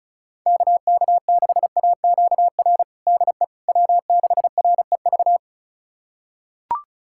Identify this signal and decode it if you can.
Morse code, heard as a keyed tone, about 36 wpm, 700 Hz. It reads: KK6AQR DE W6REV
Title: USA Callsign DE Callsign 35wpm